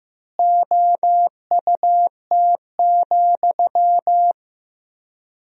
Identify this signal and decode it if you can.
Morse code, heard as a keyed tone, about 15 wpm, 700 Hz. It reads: OUT,
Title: 3x Word-Spacing - Sentences from Top 500 Words - 15wpm